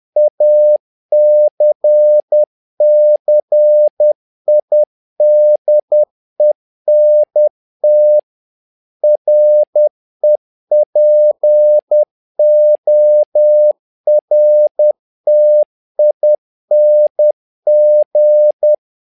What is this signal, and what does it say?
Morse code, heard as a keyed tone, about 10 wpm, 600 Hz. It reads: ACCIDENT REPORTING
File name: learning center / tower trouble / tower trouble 10wpm